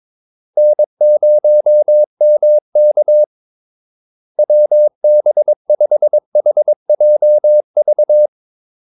Morse code, heard as a keyed tone, about 22 wpm, 600 Hz. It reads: N0MK WB5HJV
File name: Callsigns-22wpm